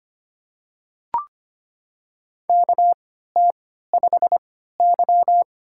Morse code, heard as a keyed tone, about 25 wpm, spaced as for 14 wpm, 700 Hz. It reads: KT5Y
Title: USA Call Signs - 4 Characters - 3x Character Spacing - 25wpm